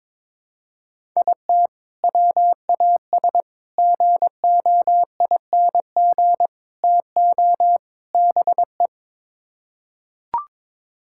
Morse code, heard as a keyed tone, about 22 wpm, 700 Hz. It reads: IT WAS GOING TO BE
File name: practice / 22wpm / Build A Phrase - 5-Word Phrases - 22wpm